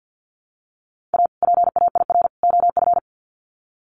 Morse code, heard as a keyed tone, about 50 wpm, 700 Hz. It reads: A PAIR OF
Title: Sets of 3 Words 50wpm